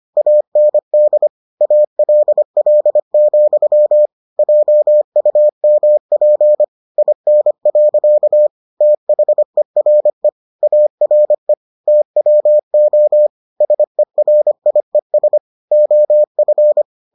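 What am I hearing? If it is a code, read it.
Morse code, heard as a keyed tone, about 25 wpm, 600 Hz. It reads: AND ALL, JUMP IN. THERE ARE TWO SERIES OF
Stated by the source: Morse edition of the Foundations of Amateur Radio episodes.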